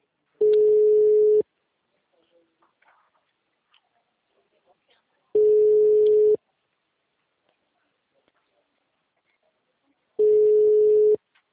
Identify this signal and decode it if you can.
Ringback tone (Europe, 425 Hz).